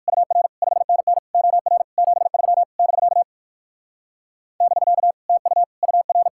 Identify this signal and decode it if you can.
Morse code, heard as a keyed tone, about 63 wpm, 700 Hz. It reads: UR 5NN OR 73 <BK> <BK> TU UR